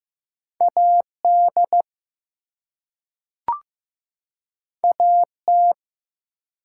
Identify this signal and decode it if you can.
Morse code, heard as a keyed tone, about 15 wpm, 700 Hz. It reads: AD AT
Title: Two Letter Words - 15wpm